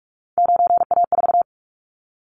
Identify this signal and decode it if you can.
Morse code, heard as a keyed tone, about 44 wpm, 700 Hz. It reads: 9A4
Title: Callsign Prefixes for Top 100 DXCC - 45wpm